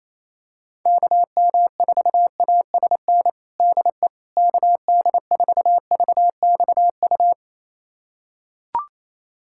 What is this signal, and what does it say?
Morse code, heard as a keyed tone, about 28 wpm, 700 Hz. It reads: KM4ASN DE KD4VXU
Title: USA Callsign DE Callsign - 6 Characters - 28wpm